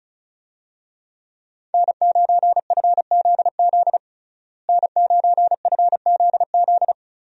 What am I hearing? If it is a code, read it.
Morse code, heard as a keyed tone, about 36 wpm, 700 Hz. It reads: N9FZZ N9FZZ
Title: USA Call Signs - 5 Characters - Repeated 3x - 35wpm